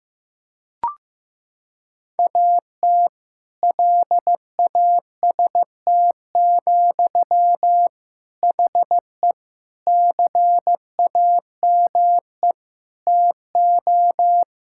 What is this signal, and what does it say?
Morse code, heard as a keyed tone, about 15 wpm, 700 Hz. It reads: AT LAST, HE CAME TO
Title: Sentences from Top 1600 Words 15wpm